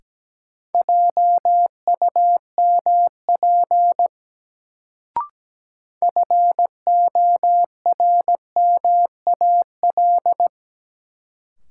Morse code, heard as a keyed tone, about 17 wpm, 700 Hz. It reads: JUMP FORMAL
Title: Sentences from Top 1300 Words - New Word Drill Alt - 17wpm